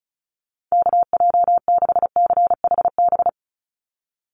Morse code, heard as a keyed tone, about 34 wpm, 700 Hz. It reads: KJ6CHB
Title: USA Call Signs 35wpm